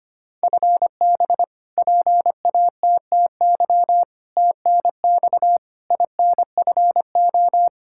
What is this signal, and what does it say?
Morse code, heard as a keyed tone, about 25 wpm, 700 Hz. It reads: FB PATTY TNX INFO
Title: K1USN SST QSO Practice 25wpm